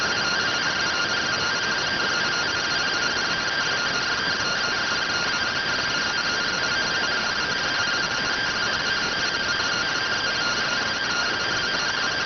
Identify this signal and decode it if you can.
Two-tone teleprinter signal, about 170 Hz shift.